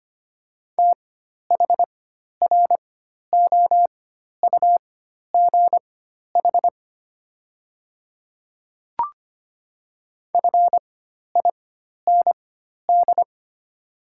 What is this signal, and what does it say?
Morse code, heard as a keyed tone, about 25 wpm, spaced as for 12 wpm, 700 Hz. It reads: THROUGH FIND
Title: 4x Character Spacing - Top 200 Words - 25wpm